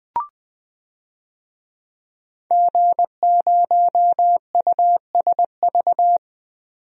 Morse code, heard as a keyed tone, about 20 wpm, 700 Hz. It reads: G0USV
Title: DXCC-100 Call Signs - 5 Characters - 20wpm